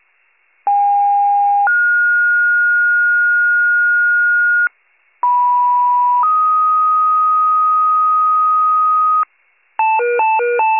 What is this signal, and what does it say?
Two-tone page, heard as a one-second tone, then a three-second tone, about 810 then 1430 Hz, about 980 then 1250 Hz, with noise. It is heard from 0.7 to 9.2 s.